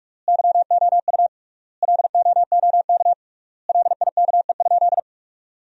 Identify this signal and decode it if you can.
Morse code, heard as a keyed tone, about 42 wpm, 700 Hz. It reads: YOU LOOK LIKE?
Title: 3x Word-Spacing - Sentences from Top 100 Words - 45wpm